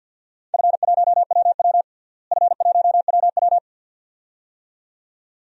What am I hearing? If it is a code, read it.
Morse code, heard as a keyed tone, about 51 wpm, 700 Hz. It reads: F1WW F1WW
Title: DXCC-100 Call Signs - 4 Characters - Repeated 2x - 50wpm